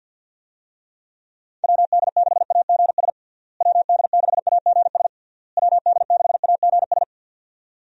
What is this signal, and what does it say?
Morse code, heard as a keyed tone, about 47 wpm, 700 Hz. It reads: WD6AGS WD6AGS WD6AGS
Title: USA Call Signs - 6 Characters - Repeated 3x - 50wpm